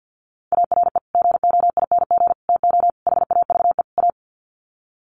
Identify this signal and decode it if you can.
Morse code, heard as a keyed tone, about 48 wpm, 700 Hz. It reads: ARE GOING TO HAVE A
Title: Sets of 5 Words Encore2 50wpm